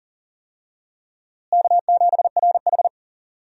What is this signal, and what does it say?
Morse code, heard as a keyed tone, about 40 wpm, 700 Hz. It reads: K7RH
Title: USA Call Signs - 4 Characters - Repeated 3x - 40wpm